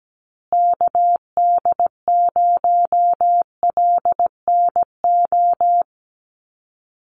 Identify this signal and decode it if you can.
Morse code, heard as a keyed tone, about 17 wpm, 700 Hz. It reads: KD0LNO
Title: USA Call Signs Repeated 3x 17wpm